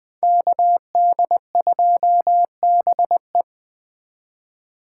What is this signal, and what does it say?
Morse code, heard as a keyed tone, about 20 wpm, 700 Hz. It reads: KD2BE
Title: USA Call Signs - 5 Characters - 4x Character Spacing - 20wpm